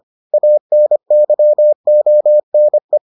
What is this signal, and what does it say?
Morse code, heard as a keyed tone, about 25 wpm, 600 Hz. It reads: ANYONE